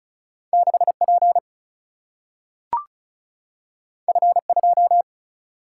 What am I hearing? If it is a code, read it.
Morse code, heard as a keyed tone, about 36 wpm, 700 Hz. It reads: 6 P F 2
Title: Single-Letter Single-Number 35wpm